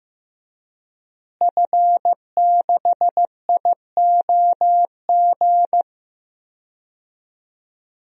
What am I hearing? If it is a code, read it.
Morse code, heard as a keyed tone, about 15 wpm, 700 Hz. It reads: F6IOG
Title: DXCC-100 Call Signs - 5 Characters - Repeated 2x - 15wpm